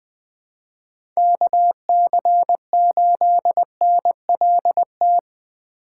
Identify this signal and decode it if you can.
Morse code, heard as a keyed tone, about 20 wpm, 700 Hz. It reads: KC8NLT
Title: USA Call Signs - 6 Characters - 4x Character Spacing - 20wpm